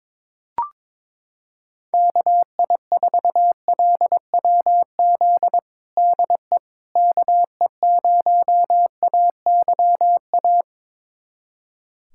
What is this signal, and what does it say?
Morse code, heard as a keyed tone, about 22 wpm, 700 Hz. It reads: KI4LWZ DE KE0AYA
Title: USA Callsign DE Callsign - 6 Characters - 22wpm